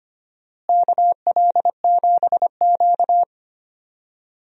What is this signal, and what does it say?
Morse code, heard as a keyed tone, about 25 wpm, 700 Hz. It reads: KL7Q
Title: DXCC-100 Call Signs - 4 Characters - Repeated 3x - 25wpm